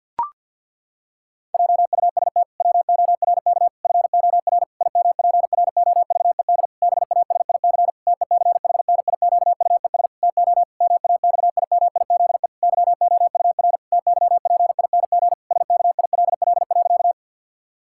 Morse code, heard as a keyed tone, about 50 wpm, 700 Hz. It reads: JUST WORK FOR IMPROVED BASIC TECHNIQUES TO MAXIMIZE YOUR TYPING SKILL.
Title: Pangram Challenge v2 - 50wpm